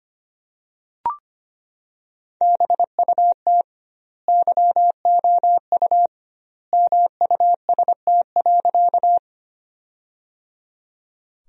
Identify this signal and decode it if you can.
Morse code, heard as a keyed tone, about 25 wpm, 700 Hz. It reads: BUT YOU MUST.
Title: 2x Word-Spacing - Sentences from Top 200 Words - 25wpm